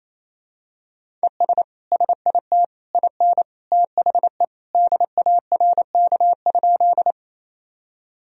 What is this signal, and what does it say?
Morse code, heard as a keyed tone, about 28 wpm, 700 Hz. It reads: ES SIT IN THE DARK?
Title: Sentences from Top 500 Words and Abbreviations 28wpm